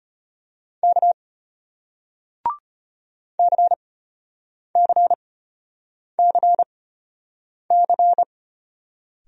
Morse code, heard as a keyed tone, about 34 wpm, spaced as for 8 wpm, 700 Hz. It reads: K CCCC